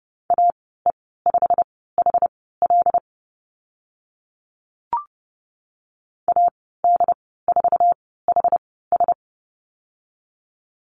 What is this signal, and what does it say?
Morse code, heard as a keyed tone, about 30 wpm, spaced as for 17 wpm, 700 Hz. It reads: AE5HL AD4HS
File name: Lesson 15g - Number 5 - Call Signs 30wpm